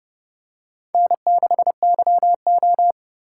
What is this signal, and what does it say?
Morse code, heard as a keyed tone, about 30 wpm, 700 Hz. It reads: N6YO
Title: USA Call Signs - 4 Characters - 8x Character Spacing - 30wpm